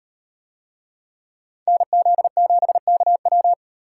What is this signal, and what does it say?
Morse code, heard as a keyed tone, about 38 wpm, 700 Hz. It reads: NZ7KW